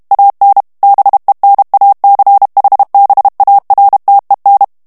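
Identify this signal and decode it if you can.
Morse code, heard as a keyed tone, about 32 wpm, 800 Hz. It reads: AN BENACHBARTEN
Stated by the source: German plain-text Morse reading practice at 30 words per minute.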